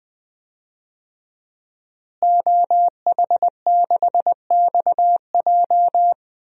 Morse code, heard as a keyed tone, about 21 wpm, 700 Hz. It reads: OH6XJ
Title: DXCC-100 Call Signs - 5 Characters - Repeated 2x - 20wpm